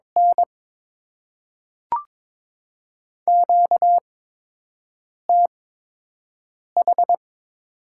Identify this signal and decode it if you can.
Morse code, heard as a keyed tone, about 22 wpm, 700 Hz. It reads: N Q T H
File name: Top 20 QSO Elements - 8x Character Spacing - 22wpm